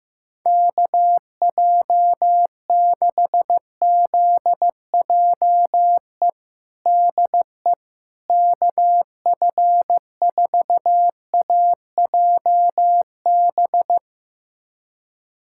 Morse code, heard as a keyed tone, about 15 wpm, 700 Hz. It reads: KJ6ZJE DE KF4AJB
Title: USA Callsign DE Callsign 15wpm